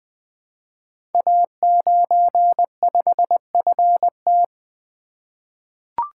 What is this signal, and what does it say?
Morse code, heard as a keyed tone, about 20 wpm, 700 Hz. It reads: A95FT
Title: Code Groups - 5 Letter:Numbers - 20wpm